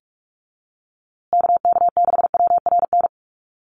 Morse code, heard as a keyed tone, about 44 wpm, 700 Hz. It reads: KK6WRN
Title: USA Call Signs Repeated 2x 45wpm